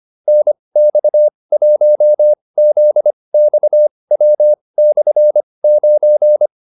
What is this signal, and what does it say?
Morse code, heard as a keyed tone, about 25 wpm, spaced as for 20 wpm, 600 Hz. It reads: NX1ZXW/9